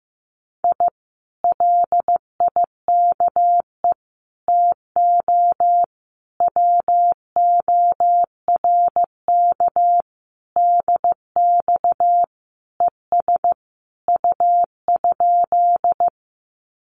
Morse code, heard as a keyed tone, about 15 wpm, 700 Hz. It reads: I LIKE TO WORK DX ES U?
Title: Common QSO Phrases Encore 15wpm